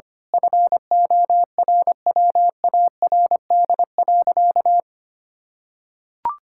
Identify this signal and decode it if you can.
Morse code, heard as a keyed tone, about 25 wpm, 700 Hz. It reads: FORWARD.
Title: Pangram Challenge v2 - 25wpm